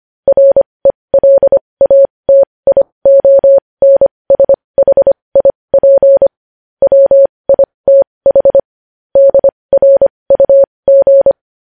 Morse code, heard as a keyed tone, about 25 wpm, spaced as for 20 wpm, 550 Hz. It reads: RELATIONSHIP WITH DRUG
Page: News Headlines in Morse Code at 20 WPM